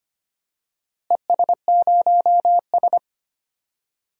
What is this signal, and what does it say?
Morse code, heard as a keyed tone, about 25 wpm, 700 Hz. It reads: ES0S
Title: DXCC-100 Call Signs - 4 Characters - Repeated 3x - 25wpm